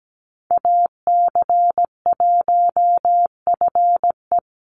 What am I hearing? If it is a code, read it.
Morse code, heard as a keyed tone, about 17 wpm, 700 Hz. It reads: AC1FE
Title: Call Signs Repeated 3x - 17wpm